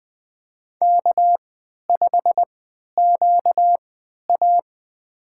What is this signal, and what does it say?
Morse code, heard as a keyed tone, about 20 wpm, spaced as for 11 wpm, 700 Hz. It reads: K5QA
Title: USA Call Signs - 4 Characters - 3x Character Spacing - 20wpm